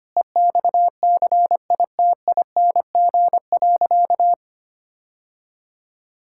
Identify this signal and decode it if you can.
Morse code, heard as a keyed tone, about 25 wpm, 700 Hz. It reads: EXCITING.
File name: practice / 25wpm / Sentences with 4-Word Phrases - 25wpm